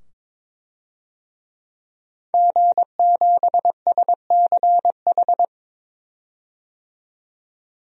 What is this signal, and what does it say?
Morse code, heard as a keyed tone, about 21 wpm, 700 Hz. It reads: G7SCH
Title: DXCC-100 Call Signs - 5 Characters - 22wpm